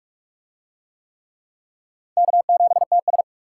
Morse code, heard as a keyed tone, about 44 wpm, 700 Hz. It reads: K7TS